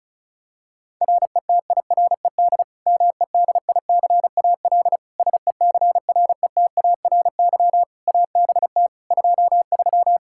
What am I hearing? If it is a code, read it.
Morse code, heard as a keyed tone, about 36 wpm, 700 Hz. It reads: RETIRED MEDICAL SECRETARY ABT 23
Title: K1USN SST QSO Practice 35wpm